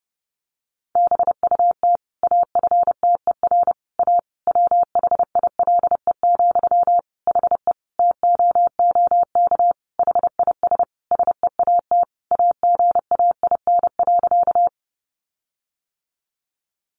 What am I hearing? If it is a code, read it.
Morse code, heard as a keyed tone, about 30 wpm, 700 Hz. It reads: BUT AFTER A WHILE, HE TOOK HIS SEAT AGAIN.
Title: Sentences from Top 1500 Words 30wpm